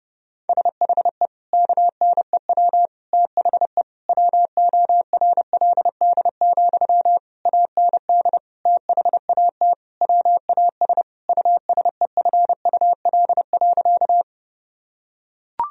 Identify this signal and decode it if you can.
Morse code, heard as a keyed tone, about 32 wpm, 700 Hz. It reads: SHE KNEW THE WORLD, AND THAT WAS USEFUL.
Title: Sentences from Top 1000 Words - Review Part 4 - 30wpm